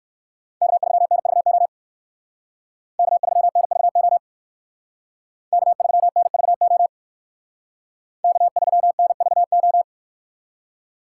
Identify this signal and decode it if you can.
Morse code, heard as a keyed tone, about 62 wpm, 700 Hz. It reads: K3NVO K3NVO K3NVO K3NVO